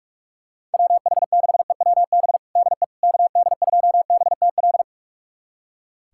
Morse code, heard as a keyed tone, about 44 wpm, 700 Hz. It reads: WH6EWB DE KD2BTL
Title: USA Callsign DE Callsign - 6 Characters - 45wpm